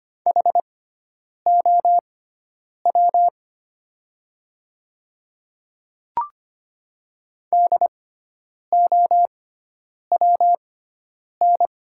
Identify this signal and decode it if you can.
Morse code, heard as a keyed tone, about 25 wpm, spaced as for 9 wpm, 700 Hz. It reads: HOW DOWN
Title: 6x Character Spacing - Top 200 Words - 25wpm